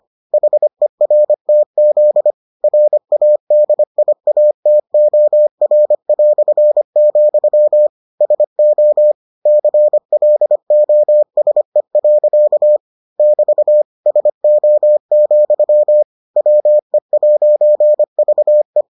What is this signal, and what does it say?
Morse code, heard as a keyed tone, about 25 wpm, 600 Hz. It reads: HERTZ RADIATOR", SO CLOSE. <BT> SO, WE'VE